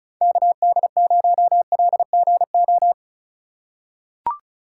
Morse code, heard as a keyed tone, about 34 wpm, 700 Hz. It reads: KD0LGO
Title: USA Call Signs - 6 Characters - 2x Character Spacing - 35wpm